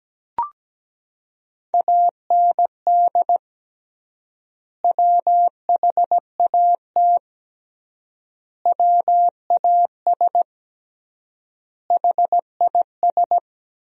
Morse code, heard as a keyed tone, about 17 wpm, 700 Hz. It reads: AND WHAT WAS HIS
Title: 3x Word-Spacing - Sentences from Top 200 Words - 17wpm